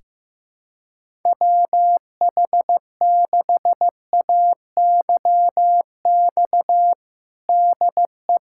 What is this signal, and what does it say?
Morse code, heard as a keyed tone, about 15 wpm, 700 Hz. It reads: WH6AYX DE
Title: USA Callsign DE Callsign 15wpm